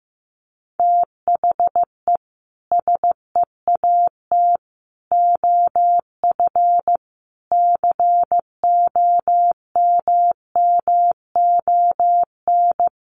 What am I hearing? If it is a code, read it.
Morse code, heard as a keyed tone, about 15 wpm, 700 Hz. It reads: THE SEAT OF COMMON
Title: Sentences from Top 1500 Words 15wpm